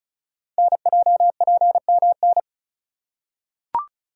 Morse code, heard as a keyed tone, about 36 wpm, 700 Hz. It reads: NJPMN
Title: Code Groups - 5 Letters - 35wpm